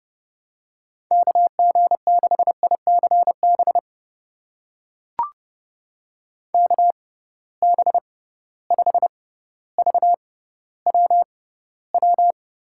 Morse code, heard as a keyed tone, about 30 wpm, spaced as for 15 wpm, 700 Hz. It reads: KG6ICB KB5VWW
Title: USA Call Signs - 6x Character Spacing - 30wpm